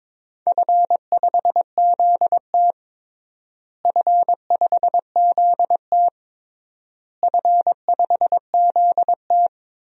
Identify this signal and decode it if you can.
Morse code, heard as a keyed tone, about 22 wpm, 700 Hz. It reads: F5ZT F5ZT F5ZT
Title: DXCC-100 Call Signs - 4 Characters - Repeated 3x - 22wpm